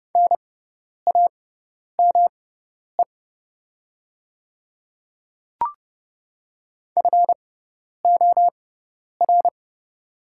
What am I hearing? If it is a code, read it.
Morse code, heard as a keyed tone, about 30 wpm, spaced as for 10 wpm, 700 Hz. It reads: NAME FOR
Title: 6x Character Spacing - Top 300 Words - 30wpm